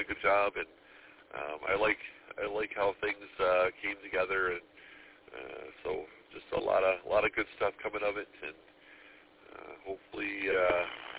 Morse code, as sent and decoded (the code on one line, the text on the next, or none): none